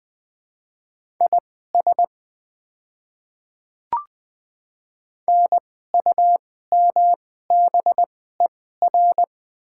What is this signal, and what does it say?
Morse code, heard as a keyed tone, about 20 wpm, spaced as for 14 wpm, 700 Hz. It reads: IS NUMBER
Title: 2x Character Spacing - Top 100 Words - 20wpm